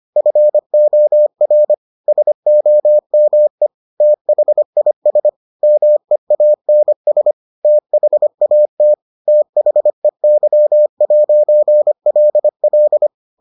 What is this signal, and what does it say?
Morse code, heard as a keyed tone, about 25 wpm, 600 Hz. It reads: FOR SOME THIS MEANS THAT THEY'LL